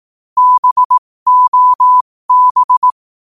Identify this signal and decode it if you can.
Morse code, heard as a keyed tone, about 18 wpm, spaced as for 15 wpm, 1000 Hz. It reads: BOB
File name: qotdfiles / 15wpm